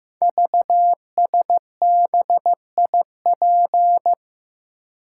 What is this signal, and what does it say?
Morse code, heard as a keyed tone, about 15 wpm, 700 Hz. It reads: VSBIP
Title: Code Groups - 5 Letter:Numbers - 15wpm